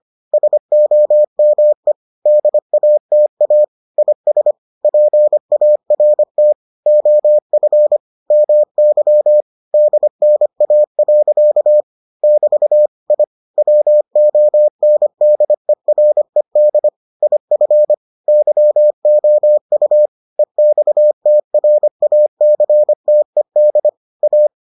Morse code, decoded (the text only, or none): SOME DATA IS PART OF MY DNA. <BT> I WONDERED IF YOU EXTRACTED A